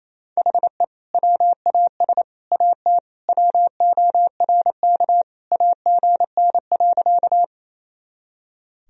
Morse code, heard as a keyed tone, about 28 wpm, 700 Hz. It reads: HE WAS AT WORK AGN.
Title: Sentences from Top 200 Words and Abbreviations 28wpm